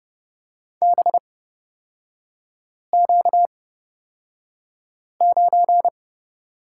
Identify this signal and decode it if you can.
Morse code, heard as a keyed tone, about 30 wpm, spaced as for 5 wpm, 690 Hz. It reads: BQ9
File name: course / 30:10wpm / 236 Lesson 38e - Letter q - Three Characters 30wpm